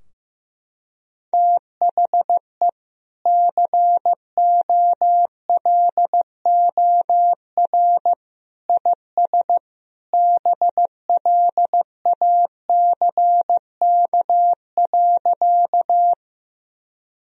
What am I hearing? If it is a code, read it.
Morse code, heard as a keyed tone, about 16 wpm, 700 Hz. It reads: THE COLOR IS BLACK.
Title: Sentences from Top 1000 Words - Review Part 2 - 15wpm